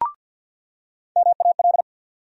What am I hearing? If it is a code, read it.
Morse code, heard as a keyed tone, about 53 wpm, 700 Hz. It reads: MAL
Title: USA Call Sign Postfixes - 3 Letters 50wpm